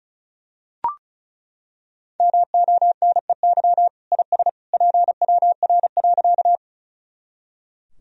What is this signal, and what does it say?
Morse code, heard as a keyed tone, about 34 wpm, 700 Hz. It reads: MONEY IS PWR.
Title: Sentences from Top 500 Words and Abbreviations 35wpm